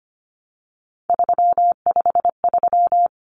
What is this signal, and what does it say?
Morse code, heard as a keyed tone, about 25 wpm, 700 Hz. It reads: 353
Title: Common Cut Numbers - 3-Digit 25wpm